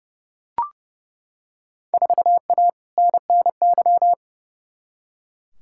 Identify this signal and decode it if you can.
Morse code, heard as a keyed tone, about 30 wpm, 700 Hz. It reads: 4A NNY
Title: Field Day Exchanges Encore - 30wpm